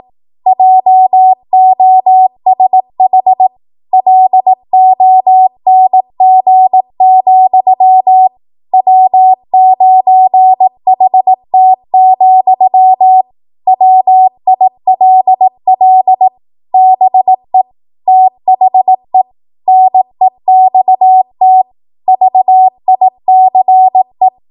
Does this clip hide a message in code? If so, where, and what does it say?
Yes: Morse code from 0.5 to 24.3 s, heard as a keyed tone, about 18 wpm, 750 Hz. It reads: JOSH LONG, W9HT, WILL BE THE NEXT VICE